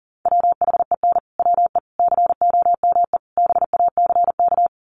Morse code, heard as a keyed tone, about 40 wpm, 700 Hz. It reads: WHEN WE COME BACK
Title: Sets of 4 Words 40wpm